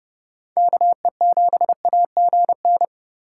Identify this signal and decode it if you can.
Morse code, heard as a keyed tone, about 39 wpm, 700 Hz. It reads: KE7AGN